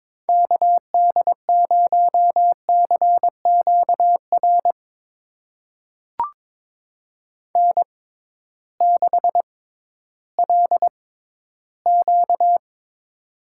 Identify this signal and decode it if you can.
Morse code, heard as a keyed tone, about 22 wpm, spaced as for 13 wpm, 700 Hz. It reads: KD0CQR N6LQ
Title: USA Call Signs - 6x Character Spacing - 22wpm